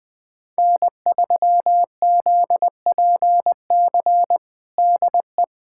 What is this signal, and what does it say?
Morse code, heard as a keyed tone, about 20 wpm, 700 Hz. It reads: N3ZPC DE
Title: USA Callsign DE Callsign - 5 Characters - 20wpm